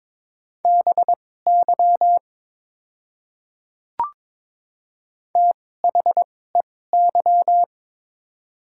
Morse code, heard as a keyed tone, about 22 wpm, spaced as for 16 wpm, 700 Hz. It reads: BY THEY
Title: 2x Character Spacing - Top 100 Words - 22wpm